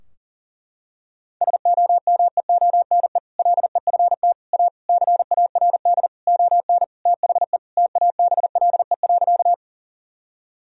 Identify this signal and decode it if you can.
Morse code, heard as a keyed tone, about 40 wpm, 700 Hz. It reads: SOMEONE LEFT A CARD ON THE TABLE.
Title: Sentences from Top 1000 Words - Review Part 4 - 40wpm